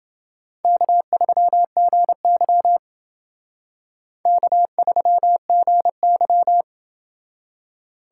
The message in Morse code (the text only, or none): K3GY K3GY